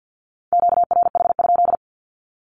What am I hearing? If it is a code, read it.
Morse code, heard as a keyed tone, about 48 wpm, 700 Hz. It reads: QRH?
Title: Q-Codes 50wpm